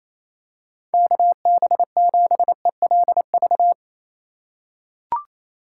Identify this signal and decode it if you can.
Morse code, heard as a keyed tone, about 28 wpm, 700 Hz. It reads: KB7ELV
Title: USA Call Signs - 6 Characters - 4x Character Spacing - 28wpm